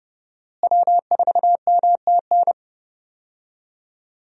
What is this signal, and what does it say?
Morse code, heard as a keyed tone, about 30 wpm, 700 Hz. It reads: W4MTN
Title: USA Call Signs - 5 Characters - Repeated 2x - 30wpm